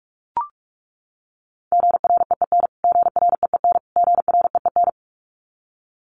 Morse code, heard as a keyed tone, about 44 wpm, 700 Hz. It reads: GREEN GREEN GREEN
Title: Top 500 Words Repeated 45wpm